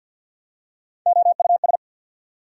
Morse code, heard as a keyed tone, about 48 wpm, 690 Hz. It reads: OUS
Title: Prefixes and Suffixes 50wpm